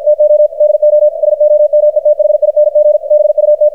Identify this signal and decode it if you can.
Morse code, heard as a keyed tone, about 52 wpm, 600 Hz, with radio noise. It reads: TO DO SOMETHING BUT
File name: archived / CRSnet